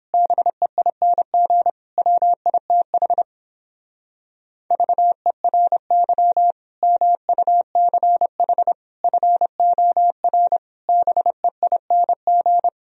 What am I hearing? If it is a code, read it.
Morse code, heard as a keyed tone, about 31 wpm, 700 Hz. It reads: BEING WITH VERY MUCH FOR BEING